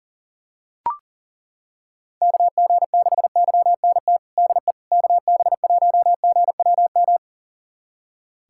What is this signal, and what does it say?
Morse code, heard as a keyed tone, about 40 wpm, 700 Hz. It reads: KG6YNT DE KB1GWM
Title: USA Callsign DE Callsign - 6 Characters - 40wpm